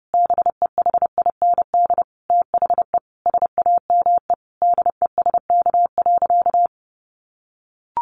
Morse code, heard as a keyed tone, about 30 wpm, 700 Hz. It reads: BEHIND THE SAME DESK.
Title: A Sound Of Thunder 30wpm